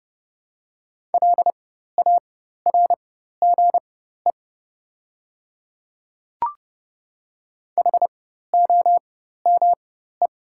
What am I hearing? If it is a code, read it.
Morse code, heard as a keyed tone, about 30 wpm, spaced as for 14 wpm, 690 Hz. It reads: LARGE HOME
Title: 4x Character Spacing - Top 500 Words - 30wpm